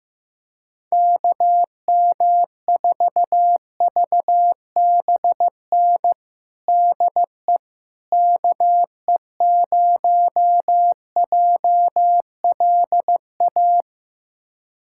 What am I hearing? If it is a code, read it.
Morse code, heard as a keyed tone, about 15 wpm, 700 Hz. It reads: KM4VBN DE KE0JLA
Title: USA Callsign DE Callsign - 6 Characters - 15wpm